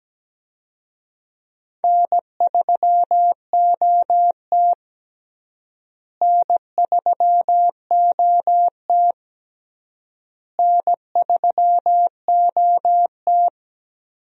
Morse code, heard as a keyed tone, about 17 wpm, 700 Hz. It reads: N3OT N3OT N3OT
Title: USA Call Signs - 4 Characters - Repeated 3x - 17wpm